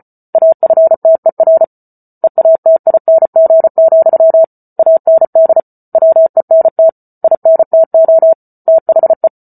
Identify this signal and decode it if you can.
Morse code, heard as a keyed tone, about 34 wpm, 650 Hz. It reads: AFTER EATING, AND WENT INTO THE